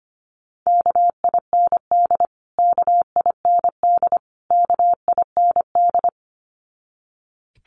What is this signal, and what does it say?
Morse code, heard as a keyed tone, about 25 wpm, 700 Hz. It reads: KIND KIND KIND
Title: Top 200 Words Repeated 25wpm